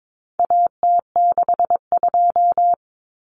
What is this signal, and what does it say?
Morse code, heard as a keyed tone, about 22 wpm, 700 Hz. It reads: AT62
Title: Common Cut Numbers - 4-Digit 22wpm